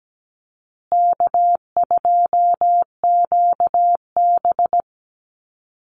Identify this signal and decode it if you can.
Morse code, heard as a keyed tone, about 17 wpm, 700 Hz. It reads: K2QB
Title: Call Signs Repeated 3x - 17wpm